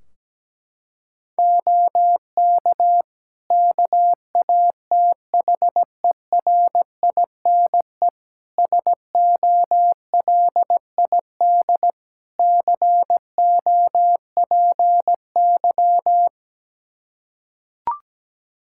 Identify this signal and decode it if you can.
Morse code, heard as a keyed tone, about 17 wpm, 700 Hz. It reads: OK KATHERINE SOLID COPY
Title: Practice QSO - Exchange 3 - More Info - 17wpm